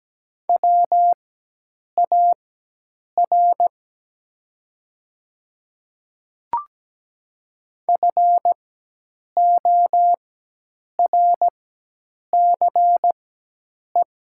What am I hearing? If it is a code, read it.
Morse code, heard as a keyed tone, about 17 wpm, spaced as for 8 wpm, 700 Hz. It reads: WAR FORCE
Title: 4x Character Spacing - Top 500 Words - 17wpm